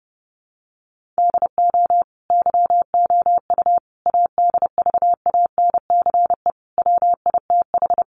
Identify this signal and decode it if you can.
Morse code, heard as a keyed tone, about 30 wpm, 700 Hz. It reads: DO YOU ADVANCE WITH
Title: Sentences from Top 1400 Words 30wpm